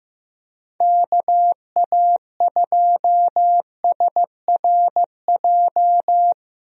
Morse code, heard as a keyed tone, about 15 wpm, 700 Hz. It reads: KA2SRJ
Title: USA Call Signs - 6 Characters - Repeated 3x - 15wpm